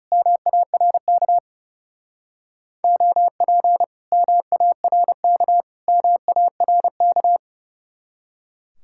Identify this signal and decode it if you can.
Morse code, heard as a keyed tone, about 36 wpm, 700 Hz. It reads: MARK OP MARK MARK
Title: Speed-Racing Common QSO Phrases Encore - 30wpm